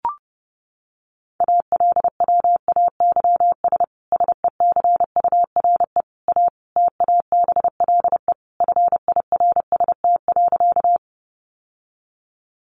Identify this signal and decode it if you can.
Morse code, heard as a keyed tone, about 30 wpm, 700 Hz. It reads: ALWAYS SECURE A TABLE FIRST.
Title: Sentences from Top 1200 Words 30wpm